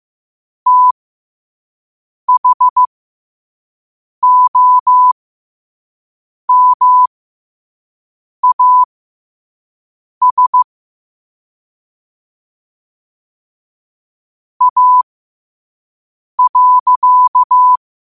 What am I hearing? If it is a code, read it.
Morse code, heard as a keyed tone, about 15 wpm, spaced as for 5 wpm, 1000 Hz. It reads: THOMAS A.
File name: qotdfiles / 5wpm